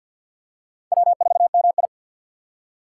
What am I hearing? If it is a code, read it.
Morse code, heard as a keyed tone, about 48 wpm, 700 Hz. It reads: W4MI
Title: USA Call Signs - 4 Characters - 6x Character Spacing - 50wpm